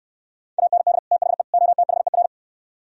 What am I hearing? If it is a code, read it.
Morse code, heard as a keyed tone, about 68 wpm, 700 Hz. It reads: AND THE OTHER